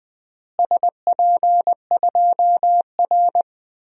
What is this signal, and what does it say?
Morse code, heard as a keyed tone, about 20 wpm, 700 Hz. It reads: SP2R
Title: DXCC-100 Call Signs - 4 Characters - Repeated 3x - 20wpm